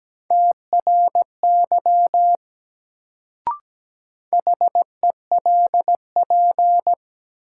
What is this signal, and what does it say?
Morse code, heard as a keyed tone, about 17 wpm, 700 Hz. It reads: TRY HELP
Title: Sentences from Top 200 Words - New Word Drill - 17wpm